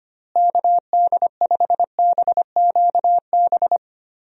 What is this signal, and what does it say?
Morse code, heard as a keyed tone, about 25 wpm, 700 Hz. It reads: KD5BQB
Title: USA Call Signs - 6 Characters - Repeated 2x - 25wpm